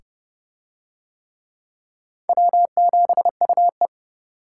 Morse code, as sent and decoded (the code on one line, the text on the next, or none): .-- --... ..- .
W7UE